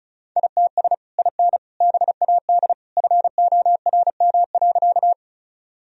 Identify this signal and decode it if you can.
Morse code, heard as a keyed tone, about 34 wpm, 700 Hz. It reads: ITS IN BAD FORM.
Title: Sentences from Top 1000 Words - Review Part 1 - 35wpm